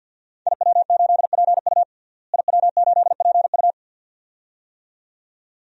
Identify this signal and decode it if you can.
Morse code, heard as a keyed tone, about 53 wpm, 700 Hz. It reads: IW8PU IW8PU
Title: DXCC-100 Call Signs - 5 Characters - Repeated 2x - 50wpm